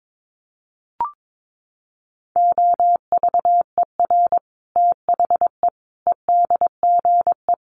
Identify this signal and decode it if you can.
Morse code, heard as a keyed tone, about 22 wpm, 700 Hz. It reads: OVER THE EDGE
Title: Sets of 5 Words Encore2 22wpm